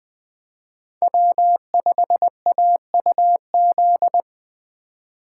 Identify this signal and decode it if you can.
Morse code, heard as a keyed tone, about 20 wpm, 700 Hz. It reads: W5AUZ